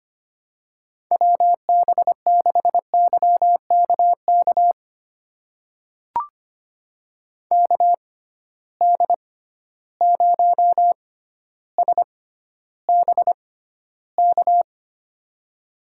Morse code, heard as a keyed tone, about 25 wpm, spaced as for 13 wpm, 700 Hz. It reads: WB6YKK KD0SBK
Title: USA Call Signs - 6 Characters - 6x Character Spacing - 25wpm